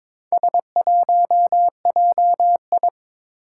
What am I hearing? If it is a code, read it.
Morse code, heard as a keyed tone, about 23 wpm, 700 Hz. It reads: S1JI
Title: Four Challenging Letter-Number Combinations 22wpm